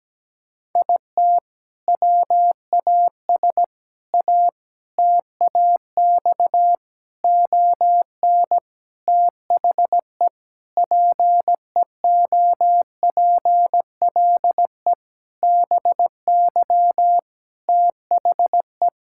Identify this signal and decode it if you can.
Morse code, heard as a keyed tone, about 17 wpm, 700 Hz. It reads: IT WAS A TAX ON THE PEOPLE BY THE
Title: Sentences from Top 1000 Words - Review Part 5 - 17wpm